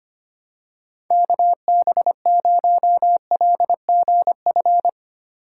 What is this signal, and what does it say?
Morse code, heard as a keyed tone, about 25 wpm, 700 Hz. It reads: KB0LGF
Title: K1USN SST QSO Practice - 25wpm